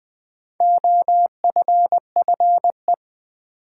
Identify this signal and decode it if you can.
Morse code, heard as a keyed tone, about 20 wpm, 690 Hz. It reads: OFFE
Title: Build A Word - 2 Letters At A Time - 6-Letter Words - 20wpm